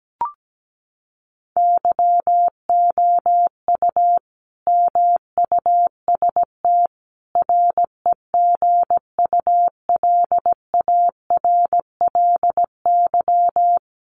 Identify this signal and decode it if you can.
Morse code, heard as a keyed tone, about 17 wpm, 700 Hz. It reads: YOU MUST REGULARLY
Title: Sentences from Top 1400 Words 17wpm